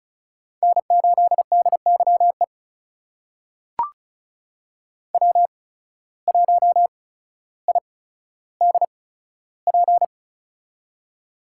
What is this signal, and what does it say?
Morse code, heard as a keyed tone, about 36 wpm, spaced as for 15 wpm, 700 Hz. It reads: N8DYE W1IDP
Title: USA Call Signs - 5 Characters - 8x Character Spacing - 35wpm